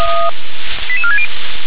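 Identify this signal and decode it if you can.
Five-tone selective call (ZVEI), heard as quick runs of five tones, tones of about 70 ms. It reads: 88460
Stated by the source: ZVEI five-tone selective call received by SDR.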